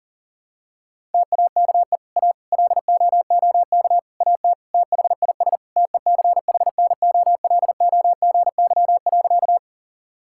Morse code, heard as a keyed tone, about 40 wpm, 700 Hz. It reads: TAKE A LOOK AT THIS TECHNOLOGY.
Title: Sentences from Top 1000 Words - Review Part 5 - 40wpm